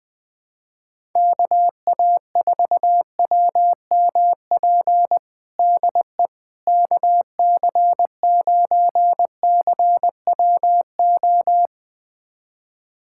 Morse code, heard as a keyed tone, about 21 wpm, 700 Hz. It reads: KA4WMP DE KC9CWO
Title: USA Callsign DE Callsign - 6 Characters - 20wpm